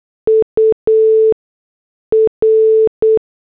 Morse code, heard as a keyed tone, about 8 wpm, spaced as for 6 wpm, 440 Hz. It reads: UR
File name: morse / headlines